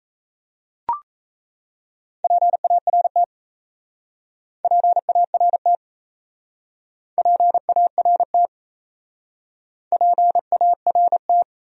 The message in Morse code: PART PART PART PART

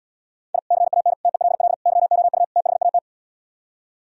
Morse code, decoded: I CAN TELL YOU THAT

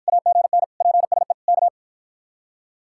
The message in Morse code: AGN PSE K

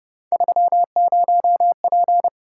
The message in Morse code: 30P

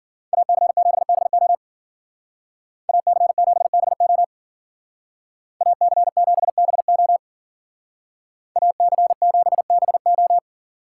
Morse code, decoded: AC7BO AC7BO AC7BO AC7BO

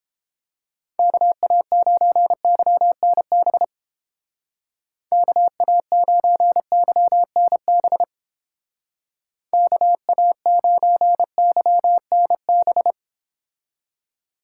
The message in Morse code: KA9YNB KA9YNB KA9YNB

